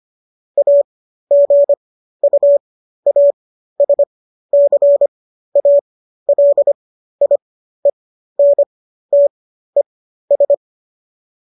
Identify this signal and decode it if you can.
Morse code, heard as a keyed tone, about 25 wpm, spaced as for 13 wpm, 570 Hz. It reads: AGUASCALIENTES